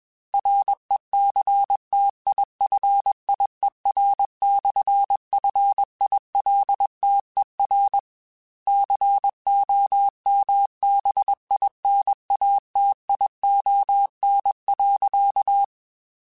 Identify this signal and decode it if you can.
Morse code, heard as a keyed tone, about 21 wpm, 800 Hz. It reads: RECTIFIER/FILTER COMBINATION.
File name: Technical Subjects / Technical Subjects-mp3s / Sw Pwr Sup @20